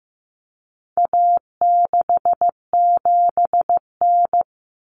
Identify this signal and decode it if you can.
Morse code, heard as a keyed tone, about 15 wpm, 700 Hz. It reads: A67N
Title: Common Cut Numbers - 4-Digit 15wpm